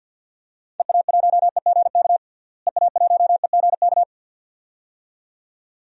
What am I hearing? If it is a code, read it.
Morse code, heard as a keyed tone, about 48 wpm, 700 Hz. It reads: EA1EGK EA1EGK
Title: DXCC-100 Call Signs - 6 Characters - Repeated 2x - 50wpm